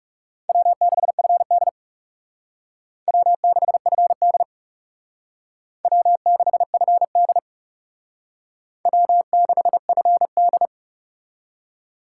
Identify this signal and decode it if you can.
Morse code, heard as a keyed tone, about 37 wpm, 700 Hz. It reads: W6FD W6FD W6FD W6FD
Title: Speed Racing - USA Call Signs 30wpm